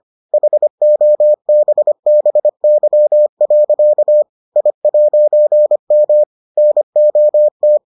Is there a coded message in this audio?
Yes: Morse code from 0.3 to 7.8 s, heard as a keyed tone, about 25 wpm, 600 Hz. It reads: HOBBY. I'M NOT